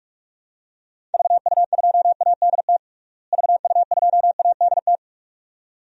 Morse code, heard as a keyed tone, about 46 wpm, 700 Hz. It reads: VU2ADT VU2ADT
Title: DXCC-100 Call Signs - 6 Characters - Repeated 2x - 45wpm